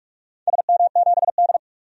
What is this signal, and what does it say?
Morse code, heard as a keyed tone, about 47 wpm, 700 Hz. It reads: SM7D
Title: DXCC-100 Call Signs - 4 Characters - 45wpm